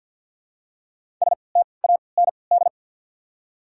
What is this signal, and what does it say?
Morse code, heard as a keyed tone, about 54 wpm, spaced as for 29 wpm, 700 Hz. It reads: STAND
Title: 3x Character Spacing - Top 300 Words - 50wpm